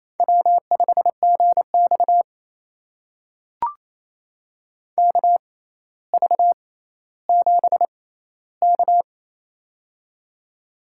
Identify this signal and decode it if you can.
Morse code, heard as a keyed tone, about 28 wpm, spaced as for 14 wpm, 700 Hz. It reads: W5GX KV7K
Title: USA Call Signs - 4 Characters - 6x Character Spacing - 28wpm